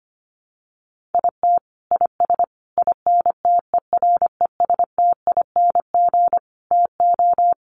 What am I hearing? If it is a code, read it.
Morse code, heard as a keyed tone, about 25 wpm, 700 Hz. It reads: IT IS INTERESTING TO
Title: Sets of 5 Words Encore2 25wpm